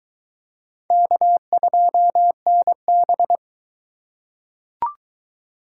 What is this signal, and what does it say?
Morse code, heard as a keyed tone, about 21 wpm, 700 Hz. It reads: K2NB